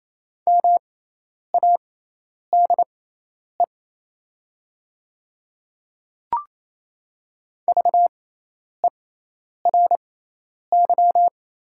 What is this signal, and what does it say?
Morse code, heard as a keyed tone, about 28 wpm, spaced as for 10 wpm, 700 Hz. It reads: MADE VERY
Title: 6x Character Spacing - Top 200 Words - 28wpm